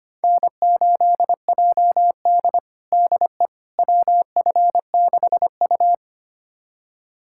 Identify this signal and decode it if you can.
Morse code, heard as a keyed tone, about 25 wpm, 700 Hz. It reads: N8JD DE WF6U
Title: USA Callsign DE Callsign - 4 Characters - 25wpm